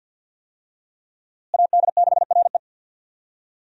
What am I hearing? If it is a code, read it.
Morse code, heard as a keyed tone, about 51 wpm, 700 Hz. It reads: AD6RE
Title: USA Call Signs - 5 Characters - Repeated 3x - 50wpm